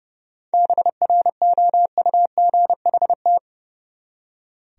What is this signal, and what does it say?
Morse code, heard as a keyed tone, about 38 wpm, 700 Hz. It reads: BROUGHT
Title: Speed-Racing Top 500 Words 30wpm